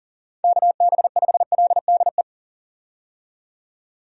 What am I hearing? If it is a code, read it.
Morse code, heard as a keyed tone, about 40 wpm, 700 Hz. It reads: KB5LDE
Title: USA Call Signs - 6 Characters - 40wpm